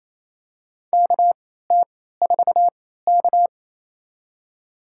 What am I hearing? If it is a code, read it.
Morse code, heard as a keyed tone, about 28 wpm, spaced as for 16 wpm, 700 Hz. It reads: KT4K
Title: USA Call Signs - 4 Characters - 3x Character Spacing - 28wpm